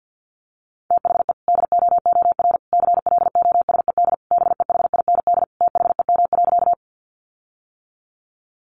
Morse code, heard as a keyed tone, about 50 wpm, 700 Hz. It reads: THE DOOR CLOSED BEHIND THEM.
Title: Sentences from Top 1400 Words 50wpm